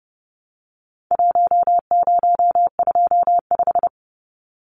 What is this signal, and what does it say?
Morse code, heard as a keyed tone, about 30 wpm, 700 Hz. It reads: 1025
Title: Numbers - Contest Style - 4-Digit 30wpm